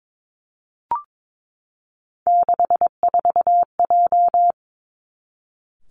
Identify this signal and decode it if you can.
Morse code, heard as a keyed tone, about 22 wpm, 700 Hz. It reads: 64J
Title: Three Challenging Letter-Number Combinations 22wpm